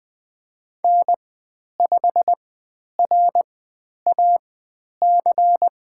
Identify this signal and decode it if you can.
Morse code, heard as a keyed tone, about 20 wpm, spaced as for 10 wpm, 700 Hz. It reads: N5RAC
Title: 095 Lesson 16g - Letter c - Call Signs 20wpm